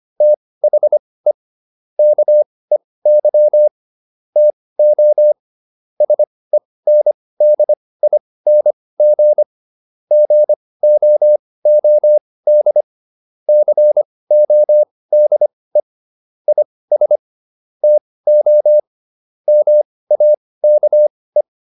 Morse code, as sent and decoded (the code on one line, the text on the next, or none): - .... / -.- . -.-- / - --- / ... -. -.. .. -. --. / --. --- --- -.. / -.-. --- -.. / .. ... / - --- / -- .- -.- .
THE KEY TO SENDING GOOD CODE IS TO MAKE